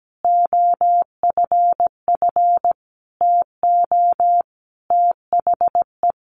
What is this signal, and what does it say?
Morse code, heard as a keyed tone, about 17 wpm, 700 Hz. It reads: OFF TO THE
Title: Sentences from Top 1400 Words 17wpm